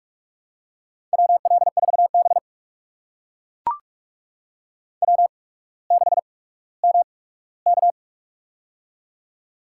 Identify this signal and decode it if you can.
Morse code, heard as a keyed tone, about 44 wpm, spaced as for 19 wpm, 700 Hz. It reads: WL4B W6MK
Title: USA Call Signs - 4 Characters - 8x Character Spacing - 45wpm